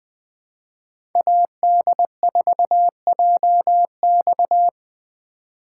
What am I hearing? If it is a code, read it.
Morse code, heard as a keyed tone, about 20 wpm, 700 Hz. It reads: AD4JX
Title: 2023 POTA Hunter Call Signs - 20wpm